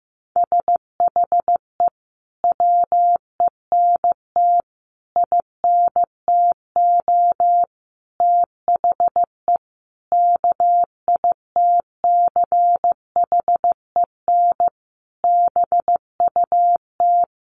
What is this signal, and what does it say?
Morse code, heard as a keyed tone, about 15 wpm, 700 Hz. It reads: SHE WENT INTO THE KITCHEN BUT
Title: Sentences from Top 1400 Words 15wpm